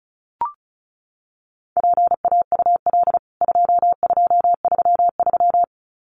Morse code, heard as a keyed tone, about 34 wpm, 700 Hz. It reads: PAUL 2233
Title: CWT Exchange 35wpm